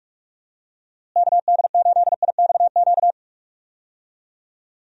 Morse code, heard as a keyed tone, about 47 wpm, 700 Hz. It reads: KD8IXQ
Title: USA Call Signs - 6 Characters - Repeated 2x - 45wpm